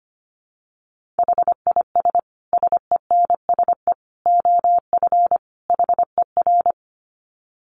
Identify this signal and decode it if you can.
Morse code, heard as a keyed tone, about 25 wpm, 700 Hz. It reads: HIS SENSE OF HER
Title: Dit Challenge 25wpm